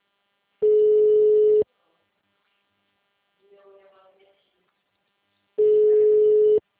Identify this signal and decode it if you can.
Ringback tone (Europe, 425 Hz), with speech.
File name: out- -2003- - - .32951